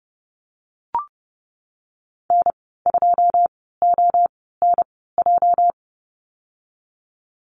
Morse code, heard as a keyed tone, about 32 wpm, spaced as for 17 wpm, 700 Hz. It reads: N2ONJ
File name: Lesson 37g - Letter x - Call Signs 30wpm